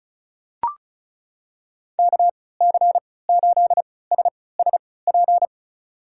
Morse code, heard as a keyed tone, about 36 wpm, spaced as for 20 wpm, 700 Hz. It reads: KC8SSP
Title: USA Call Signs - 6 Characters - 3x Character Spacing - 35wpm